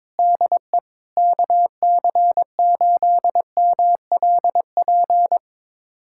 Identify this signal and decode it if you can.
Morse code, heard as a keyed tone, about 22 wpm, 700 Hz. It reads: DE KC8MLP
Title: USA Callsign DE Callsign 22wpm